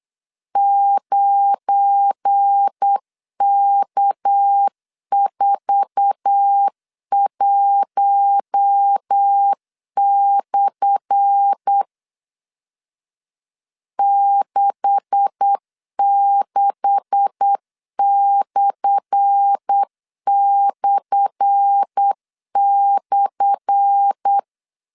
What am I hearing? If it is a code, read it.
Morse code, heard as a keyed tone, about 8 wpm, 800 Hz. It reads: 9K41/ 66///
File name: LEZIONI / 10 SIGNS / 10A SLASH 8wpm 3452